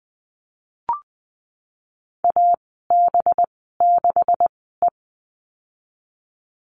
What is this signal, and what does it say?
Morse code, heard as a keyed tone, about 20 wpm, spaced as for 15 wpm, 700 Hz. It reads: AB6E
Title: 186 Lesson 30g - Letter b - Call Signs 20wpm